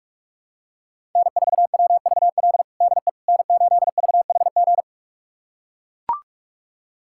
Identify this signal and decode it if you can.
Morse code, heard as a keyed tone, about 41 wpm, 700 Hz. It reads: N4WVL DE N8VHG
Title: USA Callsign DE Callsign - 5 Characters - 45wpm